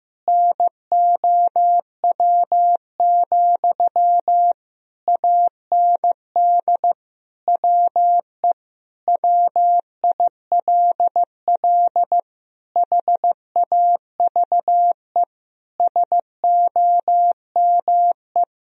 Morse code, heard as a keyed tone, about 15 wpm, 700 Hz. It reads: NOW, AND WE WILL HAVE SOME
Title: Sentences from Top 1000 Words - Review Part 4 - 15wpm